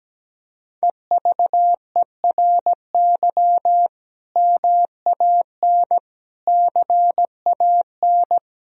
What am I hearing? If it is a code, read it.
Morse code, heard as a keyed tone, about 17 wpm, 700 Hz. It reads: EVERY MAN CAN
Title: Sentences from Top 1000 Words - Review Part 1 - 17wpm